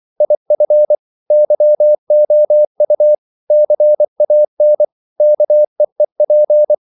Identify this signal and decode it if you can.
Morse code, heard as a keyed tone, about 24 wpm, 600 Hz. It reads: IF YOU CAN KEEP
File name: CW / Practice / If24